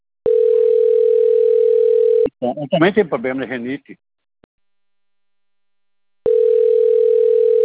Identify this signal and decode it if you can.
Ringback tone (North America), 440 + 480 Hz, with speech.
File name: rg-1600-unknown- - - .18683